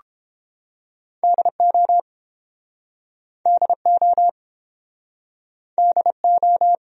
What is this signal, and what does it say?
Morse code, heard as a keyed tone, about 31 wpm, 700 Hz. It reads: DO DO DO